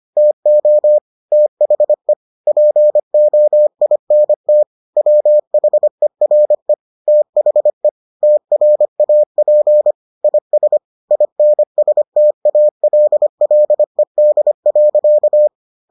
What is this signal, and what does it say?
Morse code, heard as a keyed tone, about 25 wpm, 600 Hz. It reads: TO THE POINT WHERE THE TRAP IS INSTALLED.